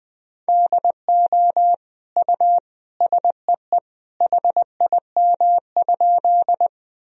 Morse code, heard as a keyed tone, about 20 wpm, 700 Hz. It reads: DO U SEE HIM?
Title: Sentences from Top 100 Words and Abbreviations 20wpm